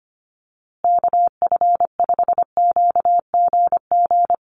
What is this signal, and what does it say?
Morse code, heard as a keyed tone, about 25 wpm, 700 Hz. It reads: KF5QGG
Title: USA Call Signs Repeated 2x 25wpm